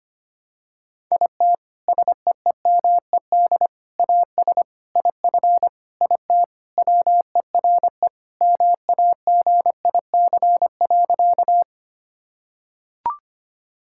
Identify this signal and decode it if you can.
Morse code, heard as a keyed tone, about 25 wpm, 700 Hz. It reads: IT SEEMED AS IF IT WERE MAGIC.
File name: practice / 25wpm / Sentences with 4-Word Phrases - 25wpm